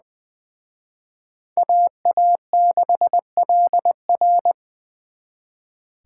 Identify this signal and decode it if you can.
Morse code, heard as a keyed tone, about 24 wpm, 700 Hz. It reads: AA6LR
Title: Speed Racing - USA 2x2 Call Signs 20wpm